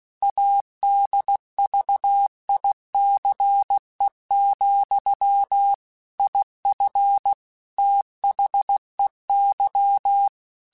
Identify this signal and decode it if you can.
Morse code, heard as a keyed tone, about 16 wpm, 800 Hz. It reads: ADVICE, IF THEY
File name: Quotes / Quotes-mp3s / Advice @15